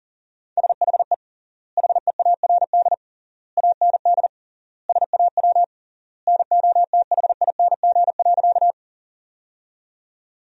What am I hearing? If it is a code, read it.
Morse code, heard as a keyed tone, about 40 wpm, 700 Hz. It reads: SHE HEARD AND SAW NOTHING.
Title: 3x Word-Spacing - Sentences from Top 500 Words - 40wpm